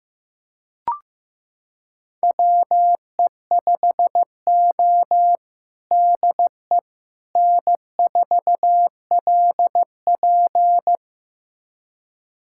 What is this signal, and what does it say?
Morse code, heard as a keyed tone, about 15 wpm, 700 Hz. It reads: WE5O DE N4LP
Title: USA Callsign DE Callsign - 4 Characters - 15wpm